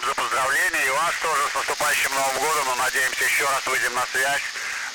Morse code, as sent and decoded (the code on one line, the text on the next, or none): none